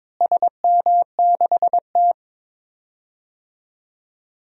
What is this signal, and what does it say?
Morse code, heard as a keyed tone, about 22 wpm, 700 Hz. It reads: SM6T
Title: DXCC-100 Call Signs - 4 Characters - Repeated 3x - 22wpm